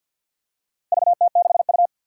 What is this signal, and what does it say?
Morse code, heard as a keyed tone, about 53 wpm, 700 Hz. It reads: VT6U
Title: Four Challenging Letter-Number Combinations 50wpm